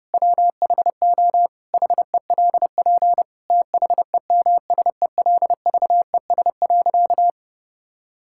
Morse code, heard as a keyed tone, about 30 wpm, 700 Hz. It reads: WHO HELP THEMSELVES.
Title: Sentences from Top 1000 Words - Review Part 5 - 30wpm